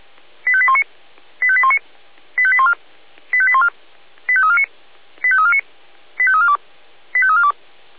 Five-tone selective call (ZVEI), heard as quick runs of five tones, tones of about 70 ms. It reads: 85819 85819 85814 85814 85349 85349 85342 85342
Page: ZVEI